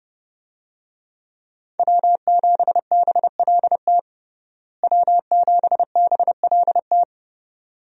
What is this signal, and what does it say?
Morse code, heard as a keyed tone, about 30 wpm, 700 Hz. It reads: W7BLT W7BLT
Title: USA Call Signs - 5 Characters - Repeated 2x - 30wpm